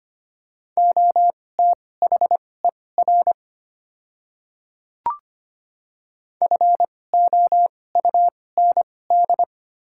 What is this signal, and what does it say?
Morse code, heard as a keyed tone, about 25 wpm, spaced as for 18 wpm, 700 Hz. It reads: OTHER FOUND
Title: 2x Character Spacing - Top 300 Words - 25wpm